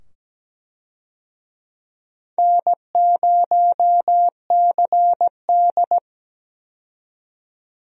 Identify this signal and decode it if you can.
Morse code, heard as a keyed tone, about 17 wpm, 700 Hz. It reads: N0CD
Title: USA Call Signs - 4 Characters - 17wpm